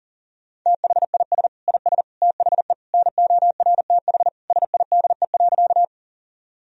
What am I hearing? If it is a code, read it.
Morse code, heard as a keyed tone, about 40 wpm, 700 Hz. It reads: THIS IS THE NORTH SIDE.
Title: Sentences from Top 1000 Words - Review Part 1 - 40wpm